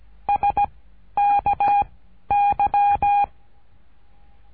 Morse code, heard as a keyed tone, about 16 wpm, 810 Hz. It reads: S K Y
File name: cursos / CURSO DE TELEGRAFIA / Telegrafia - Trilha 13